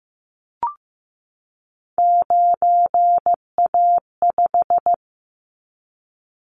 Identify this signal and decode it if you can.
Morse code, heard as a keyed tone, about 15 wpm, 700 Hz. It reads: 9A5
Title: Callsign Prefixes for Top 25 DXCC - 15wpm